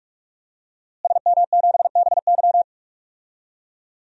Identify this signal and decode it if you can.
Morse code, heard as a keyed tone, about 45 wpm, 700 Hz. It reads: SM7BY
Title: DXCC-100 Call Signs - 5 Characters - Repeated 2x - 45wpm